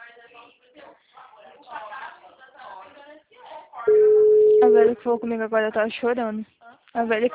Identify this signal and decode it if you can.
Ringback tone (Europe, 425 Hz), with speech.
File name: out- -2030- - - .16702